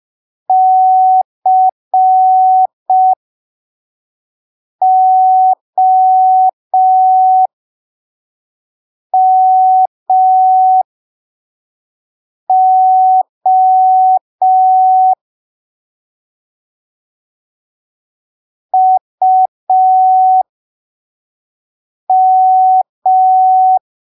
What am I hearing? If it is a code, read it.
Morse code, heard as a keyed tone, about 5 wpm, spaced as for 3 wpm, 750 Hz. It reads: COMO UM
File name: arquivos / texto / texto 01 5ppm